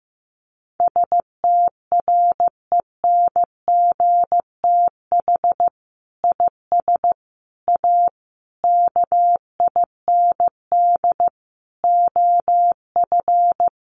Morse code, heard as a keyed tone, about 14 wpm, 700 Hz. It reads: STRENGTH IS A KIND OF
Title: Sentences from Top 1300 Words 15wpm